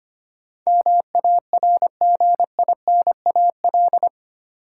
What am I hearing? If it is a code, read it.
Morse code, heard as a keyed tone, about 25 wpm, 700 Hz. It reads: MARGINAL
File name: Eight Letter Words - 25wpm